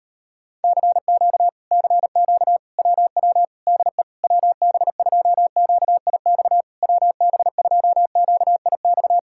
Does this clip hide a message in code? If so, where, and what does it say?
Yes: Morse code from 0.6 to 9.2 s, heard as a keyed tone, about 35 wpm, 700 Hz. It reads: CQ CQ WW DE WB2QIX WB2QIX